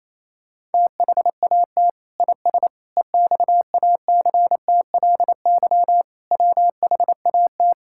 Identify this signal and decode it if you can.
Morse code, heard as a keyed tone, about 27 wpm, 700 Hz. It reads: THAT IS EXACTLY WHAT
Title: Sentences from Top 1000 Words - Review Part 6 - 28wpm